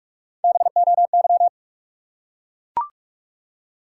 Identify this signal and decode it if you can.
Morse code, heard as a keyed tone, about 44 wpm, 700 Hz. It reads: BOY
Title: Three Letter Words - 45wpm